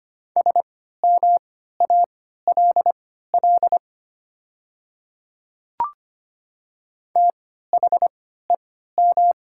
Morse code, heard as a keyed tone, about 25 wpm, spaced as for 14 wpm, 700 Hz. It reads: SMALL THEM
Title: 3x Character Spacing - Top 200 Words - 25wpm